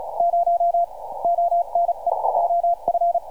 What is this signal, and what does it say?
Morse code, heard as a keyed tone, about 34 wpm, 690 Hz. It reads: 0 ONE OF
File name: archived / CRSnet